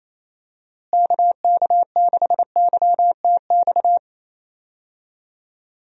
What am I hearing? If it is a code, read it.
Morse code, heard as a keyed tone, about 28 wpm, 700 Hz. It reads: KK6YTX
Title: USA Call Signs - 6 Characters - Repeated 2x - 28wpm